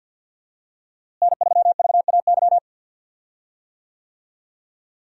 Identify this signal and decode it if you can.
Morse code, heard as a keyed tone, about 52 wpm, 700 Hz. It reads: N3VAY
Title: USA Call Signs - 5 Characters - Repeated 2x - 50wpm